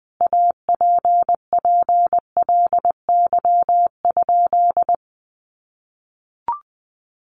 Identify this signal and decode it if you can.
Morse code, heard as a keyed tone, about 20 wpm, 700 Hz. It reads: APPLY?
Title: Sentences from Top 1200 Words 20wpm